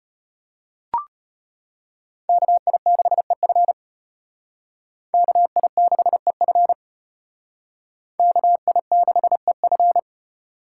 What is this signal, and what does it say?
Morse code, heard as a keyed tone, about 32 wpm, 700 Hz. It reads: KI6EF KI6EF KI6EF